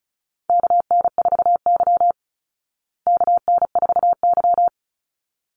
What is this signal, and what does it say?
Morse code, heard as a keyed tone, about 34 wpm, 700 Hz. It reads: KN4Y KN4Y
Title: Call Signs Repeated 3x - 35wpm